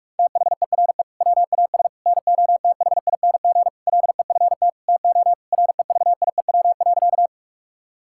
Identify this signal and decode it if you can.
Morse code, heard as a keyed tone, about 41 wpm, 700 Hz. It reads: THERE WAS NOTHING LEFT TO REVIEW.
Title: Sentences from Top 1000 Words - Review Part 6 - 45wpm